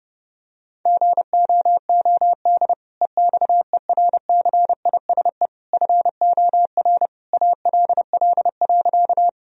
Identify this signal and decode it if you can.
Morse code, heard as a keyed tone, about 32 wpm, 700 Hz. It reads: GOOD EXERCISE FOR ALL.
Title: Sentences from Top 1000 Words - Review Part 4 - 30wpm